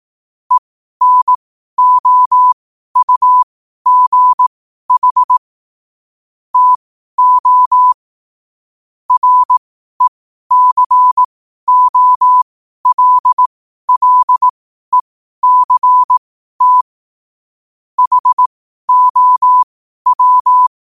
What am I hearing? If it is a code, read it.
Morse code, heard as a keyed tone, about 18 wpm, spaced as for 12 wpm, 1000 Hz. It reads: ENOUGH TO RECOLLECT HOW
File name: qotdfiles / 12wpm